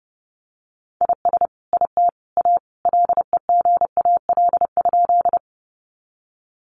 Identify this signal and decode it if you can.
Morse code, heard as a keyed tone, about 30 wpm, 700 Hz. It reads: IS IT A LEGAL?
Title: Sentences from Top 1200 Words 30wpm